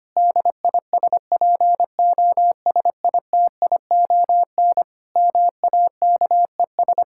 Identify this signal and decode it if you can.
Morse code, heard as a keyed tone, about 25 wpm, 700 Hz. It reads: DISPOSITION MAKES
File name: practice / 25wpm / Difficult Sentences with Hints - 25wpm